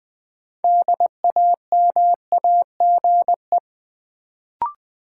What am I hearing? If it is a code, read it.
Morse code, heard as a keyed tone, about 21 wpm, 700 Hz. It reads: DAMAGE
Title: Sentences from Top 1100 Words - New Word Drill Alt - 20wpm